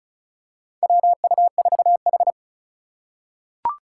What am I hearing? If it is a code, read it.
Morse code, heard as a keyed tone, about 36 wpm, 700 Hz. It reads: WU4H
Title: USA Call Signs - 3x Character Spacing - 35wpm